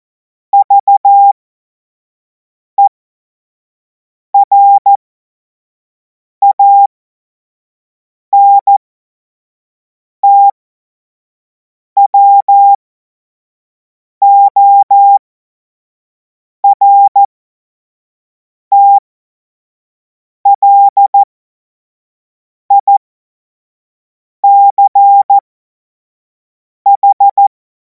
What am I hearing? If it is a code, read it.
Morse code, heard as a keyed tone, about 14 wpm, spaced as for 5 wpm, 800 Hz. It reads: VERANTWORTLICH
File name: Morsekurs / Witze / Witz 24. Februar 2026 25 F70